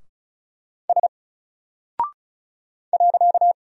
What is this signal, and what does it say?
Morse code, heard as a keyed tone, about 36 wpm, 700 Hz. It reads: S .
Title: Single Character (Letter:Number:Punctuation:Prosign) Alt 35wpm